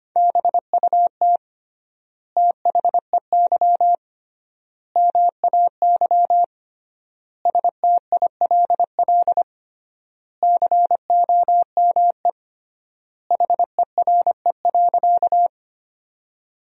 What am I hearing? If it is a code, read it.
Morse code, heard as a keyed tone, about 25 wpm, 700 Hz. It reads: BUT THEY MAY STILL COME HERE.
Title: 3x Word-Spacing - Sentences from Top 300 Words - 25wpm